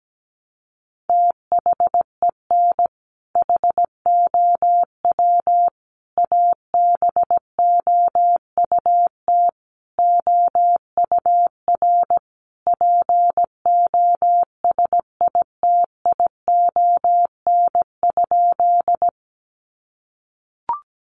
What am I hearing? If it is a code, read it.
Morse code, heard as a keyed tone, about 17 wpm, 700 Hz. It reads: THEN HOW ABOUT OUR POSITION?
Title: Sentences from Top 900 Words 17wpm